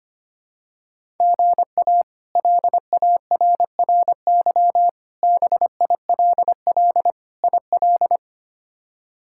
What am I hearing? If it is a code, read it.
Morse code, heard as a keyed tone, about 25 wpm, 700 Hz. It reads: GA LARRY BILL IL
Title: K1USN SST QSO Practice - 25wpm